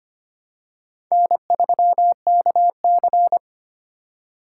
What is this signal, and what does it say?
Morse code, heard as a keyed tone, about 25 wpm, 700 Hz. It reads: N3KC
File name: practice / 25wpm / Four Letter-Number Combinations - 25wpm